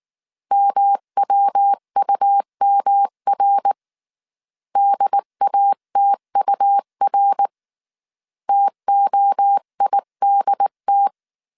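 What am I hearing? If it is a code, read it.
Morse code, heard as a keyed tone, about 19 wpm, 800 Hz. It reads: MWUMR DATUR TOIDT
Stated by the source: Morse code dictation at 18 wpm from the IW3SGT QRS Morse code training course, five-character groups of 300 characters.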